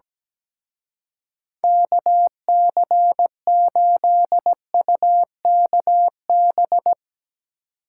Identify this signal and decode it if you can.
Morse code, heard as a keyed tone, about 19 wpm, 700 Hz. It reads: KC8UKB
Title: Speed Racing - USA 2x3 Call Signs 17wpm